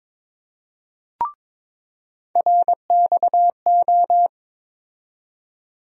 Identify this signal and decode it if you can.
Morse code, heard as a keyed tone, about 22 wpm, 700 Hz. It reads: RXO
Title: USA Call Sign Postfixes - 3 Letters 22wpm